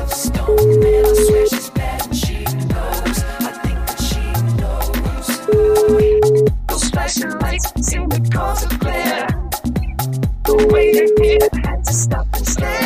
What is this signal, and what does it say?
Ringback tone (Europe, 425 Hz), with noise.